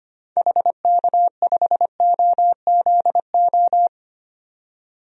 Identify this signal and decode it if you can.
Morse code, heard as a keyed tone, about 25 wpm, 700 Hz. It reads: HK5OZO
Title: DXCC-100 Call Signs - 6 Characters - 25wpm